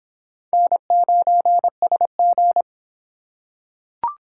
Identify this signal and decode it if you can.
Morse code, heard as a keyed tone, about 23 wpm, 700 Hz. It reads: N9SG